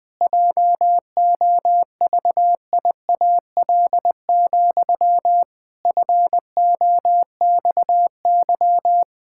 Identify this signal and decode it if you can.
Morse code, heard as a keyed tone, about 20 wpm, 700 Hz. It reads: JOVIAL, FOXY
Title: Pangram Challenge v2 - 20wpm